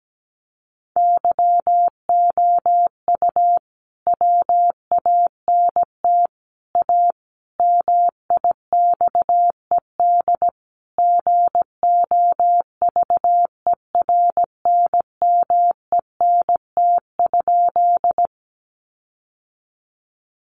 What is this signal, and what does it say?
Morse code, heard as a keyed tone, about 17 wpm, 700 Hz. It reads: YOU WANT A MIXED GOVERNMENT?
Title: Sentences from Top 1500 Words 17wpm